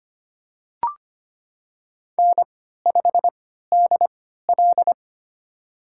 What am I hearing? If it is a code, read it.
Morse code, heard as a keyed tone, about 25 wpm, spaced as for 14 wpm, 700 Hz. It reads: N5DL
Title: USA Call Signs - 3x Character Spacing - 25wpm